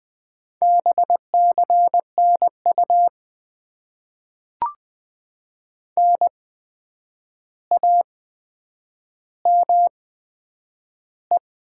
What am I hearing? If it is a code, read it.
Morse code, heard as a keyed tone, about 20 wpm, spaced as for 9 wpm, 700 Hz. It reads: BCNU NAME
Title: Top 90 QSO Elements - 8x Character Spacing - 20wpm